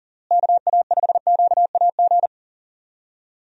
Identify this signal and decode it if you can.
Morse code, heard as a keyed tone, about 37 wpm, 700 Hz. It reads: KA5QAG
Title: USA Call Signs - 6 Characters - Repeated 3x - 40wpm